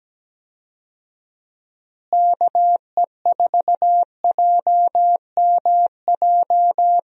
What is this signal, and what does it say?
Morse code, heard as a keyed tone, about 17 wpm, 700 Hz. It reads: KE4JMJ
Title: USA Call Signs - 6 Characters - Repeated 2x - 17wpm